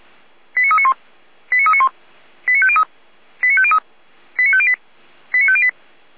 Five-tone selective call (ZVEI), heard as quick runs of five tones, tones of about 70 ms. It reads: 89391 89391 89503 89503 89508 89508